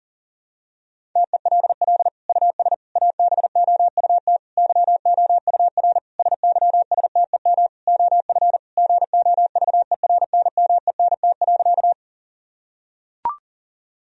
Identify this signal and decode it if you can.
Morse code, heard as a keyed tone, about 40 wpm, 700 Hz. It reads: TELL US ABOUT YOUR SYSTEM OF GOVERNMENT.
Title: Sentences from Top 1000 Words - Review Part 3 - 40wpm